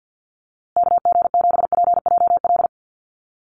Morse code, heard as a keyed tone, about 49 wpm, 700 Hz. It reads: KG7PJL
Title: USA Call Signs Repeated 3x 50wpm